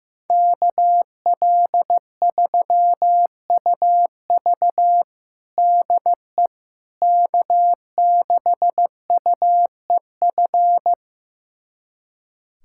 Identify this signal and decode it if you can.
Morse code, heard as a keyed tone, about 16 wpm, 700 Hz. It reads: KL3UV DE K6UEF
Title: USA Callsign DE Callsign - 5 Characters - 15wpm